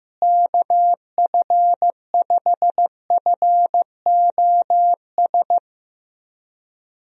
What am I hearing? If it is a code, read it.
Morse code, heard as a keyed tone, about 15 wpm, 700 Hz. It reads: KF5FOS
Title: USA Call Signs - 6 Characters - Repeated 3x - 15wpm